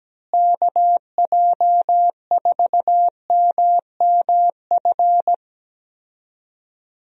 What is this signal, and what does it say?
Morse code, heard as a keyed tone, about 17 wpm, 700 Hz. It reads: KJ4MMF
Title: USA Call Signs - 6 Characters - Repeated 3x - 17wpm